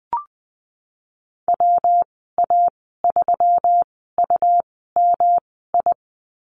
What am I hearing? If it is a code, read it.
Morse code, heard as a keyed tone, about 20 wpm, spaced as for 15 wpm, 700 Hz. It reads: WA3UMI
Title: 122 Lesson 20g - Number 3 - Call Signs 20wpm